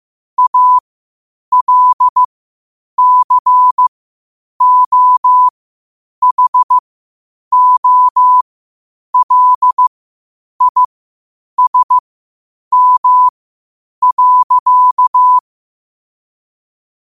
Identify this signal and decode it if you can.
Morse code, heard as a keyed tone, about 15 wpm, spaced as for 8 wpm, 1000 Hz. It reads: ALCOHOLISM.
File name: qotdfiles / 8wpm